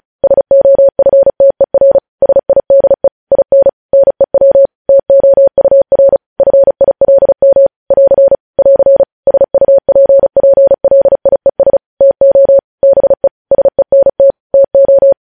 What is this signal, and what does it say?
Morse code, heard as a keyed tone, about 36 wpm, 550 Hz. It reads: SOFTER SIDE IN NEW TOUR FILM <AR> <AR> SUPPLIES TO BE SENT TO
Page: news headlines in morse code at 35 wpm